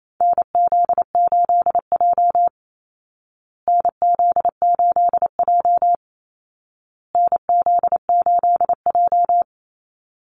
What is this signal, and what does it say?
Morse code, heard as a keyed tone, about 28 wpm, 700 Hz. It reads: NZ8J NZ8J NZ8J
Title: Call Signs Repeated 2x - 28wpm